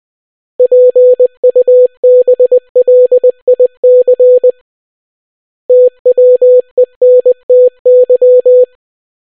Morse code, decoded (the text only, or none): PUBLIC TWENTY